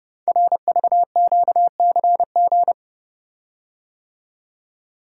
Morse code, heard as a keyed tone, about 30 wpm, 700 Hz. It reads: RVQCG
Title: Code Groups - 5 Letters - 30wpm